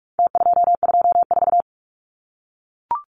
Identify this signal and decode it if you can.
Morse code, heard as a keyed tone, about 44 wpm, 700 Hz. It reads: T224
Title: Numbers - Contest Style - 4-Digit 45wpm